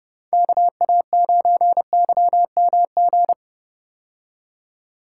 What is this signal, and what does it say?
Morse code, heard as a keyed tone, about 32 wpm, 700 Hz. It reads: KA9YMG
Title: K1USN SST QSO Practice - 30wpm